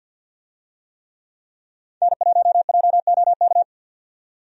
Morse code, heard as a keyed tone, about 52 wpm, 700 Hz. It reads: N1JOK
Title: USA Call Signs - 5 Characters - 50wpm